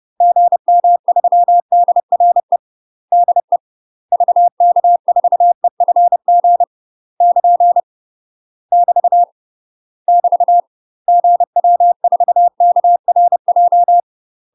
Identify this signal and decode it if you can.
Morse code, heard as a keyed tone, about 30 wpm, 700 Hz. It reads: GM3DRE DE VK4EFG <KN> <BT> <BT> GW4KRJ
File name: UK MASTER QSOs TRAINING / 30WPM / UK MASTER QSOs 0000